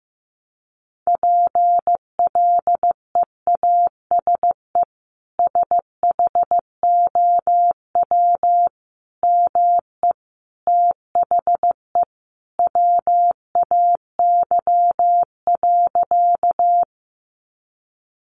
Morse code, heard as a keyed tone, about 15 wpm, 700 Hz. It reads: PLEASE SHOW ME THE WAY.
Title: Sentences from Top 1500 Words 15wpm